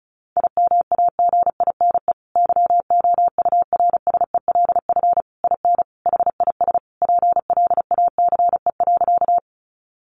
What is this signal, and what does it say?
Morse code, heard as a keyed tone, about 34 wpm, 700 Hz. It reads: IMAGINE YOURSELF IN HIS PLACE.
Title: Sentences from Top 1400 Words 35wpm